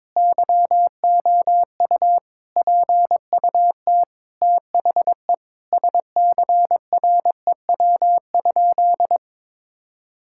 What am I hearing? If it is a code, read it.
Morse code, heard as a keyed tone, about 22 wpm, 700 Hz. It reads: YOU PUT THE SCREW?
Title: Sentences from Top 1600 Words 22wpm